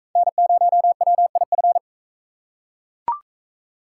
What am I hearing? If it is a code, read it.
Morse code, heard as a keyed tone, about 40 wpm, 700 Hz. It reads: N0WIF